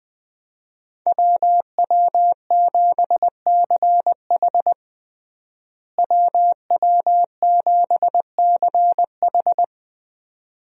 Morse code, heard as a keyed tone, about 21 wpm, 700 Hz. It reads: WW7CH WW7CH
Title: USA Call Signs - 5 Characters - Repeated 2x - 20wpm